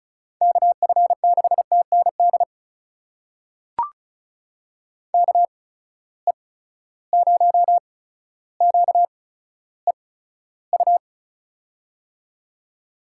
Morse code, heard as a keyed tone, about 36 wpm, spaced as for 15 wpm, 700 Hz. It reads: KF6TND KE0QEU
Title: USA Call Signs - 6 Characters - 8x Character Spacing - 35wpm